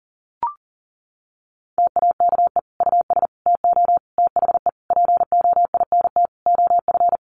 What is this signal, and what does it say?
Morse code, heard as a keyed tone, about 37 wpm, 700 Hz. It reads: TAKE US TO THE POINT OF
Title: Sentences from Top 1500 Words 40wpm